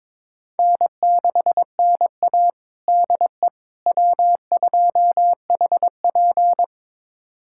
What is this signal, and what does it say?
Morse code, heard as a keyed tone, about 22 wpm, 700 Hz. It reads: N6NA DE W2HP
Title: USA Callsign DE Callsign - 4 Characters - 22wpm